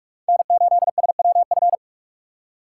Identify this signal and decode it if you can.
Morse code, heard as a keyed tone, about 43 wpm, 690 Hz. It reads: N8SWF
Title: USA Call Signs - 4x Character Spacing - 45wpm